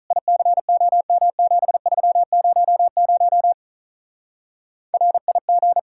Morse code, heard as a keyed tone, about 44 wpm, 700 Hz. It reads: ICOM7300 RIG